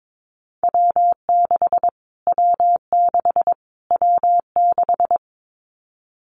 Morse code, heard as a keyed tone, about 22 wpm, 700 Hz. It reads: W6 W6 W6
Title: Callsign Prefixes for Top 25 DXCC - Repeated 3x - 22wpm